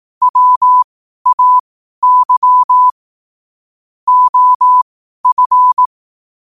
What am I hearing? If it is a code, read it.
Morse code, heard as a keyed tone, about 18 wpm, spaced as for 12 wpm, 1000 Hz. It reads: WAY OF
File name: qotdfiles / 12wpm